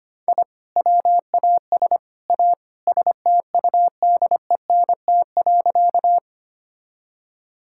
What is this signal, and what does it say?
Morse code, heard as a keyed tone, about 25 wpm, 700 Hz. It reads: I WAS A STUDENT.
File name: practice / 25wpm / Sentences with 3-Word Phrases - 25wpm